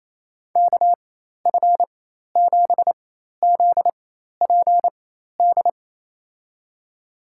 Morse code, heard as a keyed tone, about 28 wpm, spaced as for 13 wpm, 700 Hz. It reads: KF7ZPD
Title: USA Call Signs - 4x Character Spacing - 28wpm